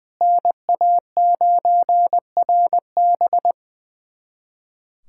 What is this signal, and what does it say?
Morse code, heard as a keyed tone, about 21 wpm, 700 Hz. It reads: NA9RB
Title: USA Callsign DE Callsign - 5 Characters - 20wpm